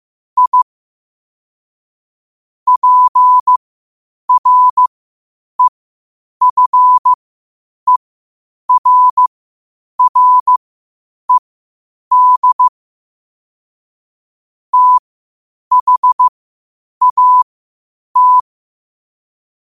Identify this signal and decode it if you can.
Morse code, heard as a keyed tone, about 15 wpm, spaced as for 8 wpm, 1000 Hz. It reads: I PREFERRED THAT
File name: qotdfiles / 8wpm